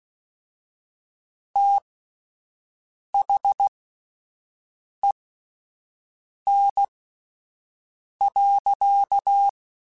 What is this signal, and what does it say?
Morse code, heard as a keyed tone, about 16 wpm, spaced as for 5 wpm, 800 Hz. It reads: THEN.
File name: (spaced) / QSOs Spaced-mp3s / WA7QOX 1 @10 sp